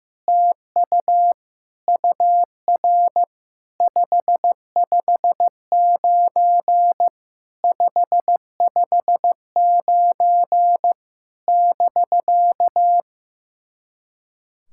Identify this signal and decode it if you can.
Morse code, heard as a keyed tone, about 16 wpm, 700 Hz. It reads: TU UR 559 559 <BK>
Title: SOTA QSO Practice 15wpm